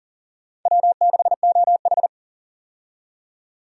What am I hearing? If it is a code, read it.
Morse code, heard as a keyed tone, about 40 wpm, 700 Hz. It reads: W6OH
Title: USA Call Signs - 4 Characters - 40wpm